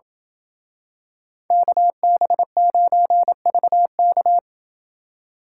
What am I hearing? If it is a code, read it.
Morse code, heard as a keyed tone, about 27 wpm, 700 Hz. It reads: KB9VK